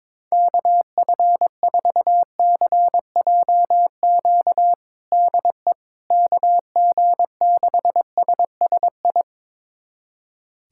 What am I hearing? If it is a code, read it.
Morse code, heard as a keyed tone, about 21 wpm, 700 Hz. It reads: KF4CJQ DE KG6SSI
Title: USA Callsign DE Callsign - 6 Characters - 22wpm